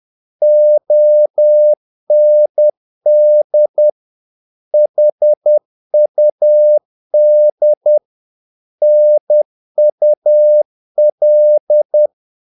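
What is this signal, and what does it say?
Morse code, heard as a keyed tone, about 10 wpm, 600 Hz. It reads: OND HUD NUL
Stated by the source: Morse practice of short Danish words at 10 wpm, from CW med Gnister.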